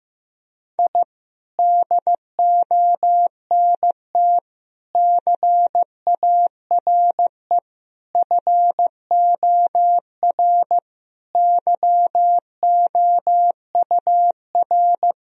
Morse code, heard as a keyed tone, about 14 wpm, 700 Hz. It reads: I DONT CARE FOR YOUR
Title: Sentences from Top 1000 Words - Review Part 3 - 15wpm